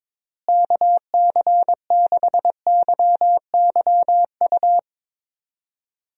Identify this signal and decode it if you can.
Morse code, heard as a keyed tone, about 21 wpm, 700 Hz. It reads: KC6YYU
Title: USA Call Signs - 6 Characters - Repeated 2x - 22wpm